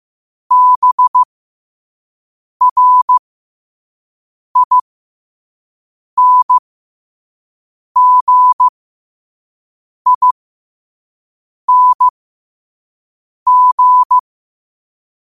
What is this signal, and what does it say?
Morse code, heard as a keyed tone, about 15 wpm, spaced as for 5 wpm, 1000 Hz. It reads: BRINGING
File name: qotdfiles / 5wpm